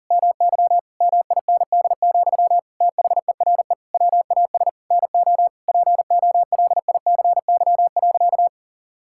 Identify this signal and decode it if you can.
Morse code, heard as a keyed tone, about 37 wpm, 700 Hz. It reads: MY MIND, THERE WAS NO POLICY.
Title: Sentences from Top 1000 Words - Review Part 5 - 40wpm